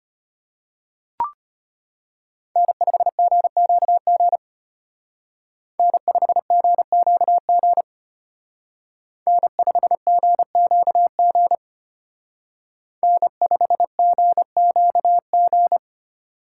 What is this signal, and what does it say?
Morse code, heard as a keyed tone, about 32 wpm, 700 Hz. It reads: N5GQG N5GQG N5GQG N5GQG